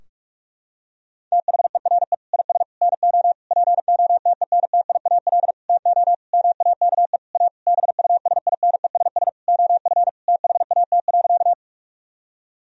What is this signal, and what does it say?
Morse code, heard as a keyed tone, about 44 wpm, 700 Hz. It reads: THERE IS NO POTENTIAL TO MAKE A BUSINESS OF THAT.
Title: Sentences from Top 1000 Words - Review Part 4 - 45wpm